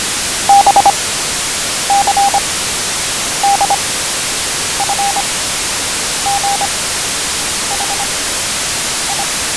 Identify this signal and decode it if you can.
Morse code, heard as a keyed tone, about 25 wpm, 800 Hz, with radio noise. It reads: B C D F G H I